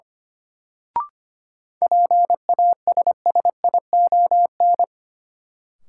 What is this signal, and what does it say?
Morse code, heard as a keyed tone, about 25 wpm, 700 Hz. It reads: PASSION
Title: Sentences from Top 1300 Words - New Word Drill - 25wpm